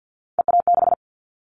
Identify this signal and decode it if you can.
Morse code, heard as a keyed tone, about 50 wpm, 700 Hz. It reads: EA6
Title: Callsign Prefixes for Top 50 DXCC - 50wpm